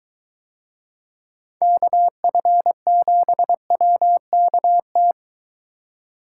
Morse code, heard as a keyed tone, about 22 wpm, 700 Hz. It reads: KF7WKT